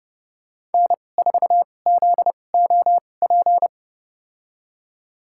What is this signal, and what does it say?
Morse code, heard as a keyed tone, about 30 wpm, spaced as for 22 wpm, 700 Hz. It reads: N4ZOP
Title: USA Call Signs - 5 Characters - 2x Character Spacing - 30wpm